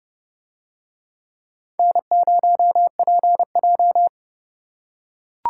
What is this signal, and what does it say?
Morse code, heard as a keyed tone, about 28 wpm, 700 Hz. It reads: N0PJ